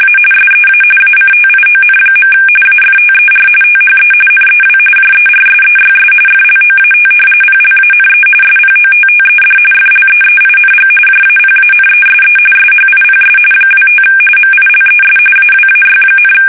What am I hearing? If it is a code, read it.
RTTY, about 45.45 baud. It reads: A SEA ACTION. AFTER THE NEW JERSEYS FLOAT PLANE SPOTTED FLEEING JAPANESE SHIPS 25 MILES AWAY,
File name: fsk850 / battleshipnj